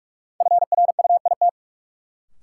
Morse code, heard as a keyed tone, about 46 wpm, 700 Hz. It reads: FRUIT
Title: Sentences from Top 1300 Words - New Word Drill - 45wpm